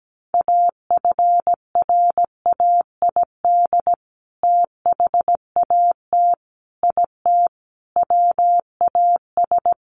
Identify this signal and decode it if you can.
Morse code, heard as a keyed tone, about 17 wpm, 700 Hz. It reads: AFRAID THAT IT WAS
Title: Sentences from Top 1500 Words 17wpm